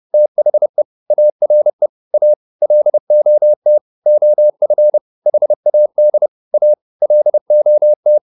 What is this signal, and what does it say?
Morse code, heard as a keyed tone, about 30 wpm, 600 Hz. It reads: THE ARE A LOT OF HAD A LOT